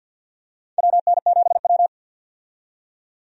Morse code, heard as a keyed tone, about 48 wpm, 700 Hz. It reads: WN7W
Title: USA Call Signs - 4 Characters - 50wpm